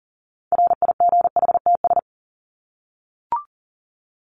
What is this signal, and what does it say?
Morse code, heard as a keyed tone, about 40 wpm, 700 Hz. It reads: RIGHTS
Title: ES:S Suffix - Top 100 Words with Base Words 40wpm